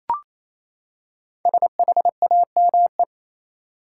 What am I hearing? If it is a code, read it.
Morse code, heard as a keyed tone, about 28 wpm, 700 Hz. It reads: SHAME
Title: Five Letter Words - 28wpm